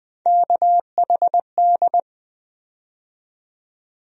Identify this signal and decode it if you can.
Morse code, heard as a keyed tone, about 20 wpm, 700 Hz. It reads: KHD
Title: USA Call Sign Postfixes - 3 Letters 20wpm